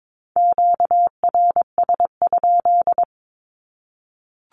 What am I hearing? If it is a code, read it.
Morse code, heard as a keyed tone, about 22 wpm, 700 Hz. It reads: QRS?
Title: Q-Codes 22wpm